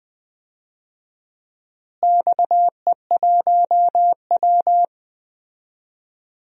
Morse code, heard as a keyed tone, about 21 wpm, 700 Hz. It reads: XE1W
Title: DXCC-100 Call Signs - 4 Characters - Repeated 2x - 20wpm